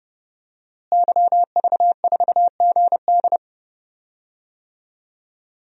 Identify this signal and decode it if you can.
Morse code, heard as a keyed tone, about 30 wpm, 700 Hz. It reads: YV4GD
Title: DXCC-100 Call Signs - 5 Characters - Repeated 2x - 30wpm